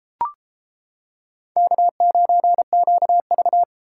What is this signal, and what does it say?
Morse code, heard as a keyed tone, about 33 wpm, 700 Hz. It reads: K9QV